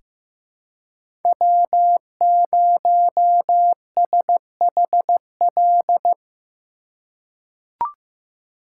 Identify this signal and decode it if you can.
Morse code, heard as a keyed tone, about 15 wpm, 700 Hz. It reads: W0SHL
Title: K1USN SST QSO Practice - 15wpm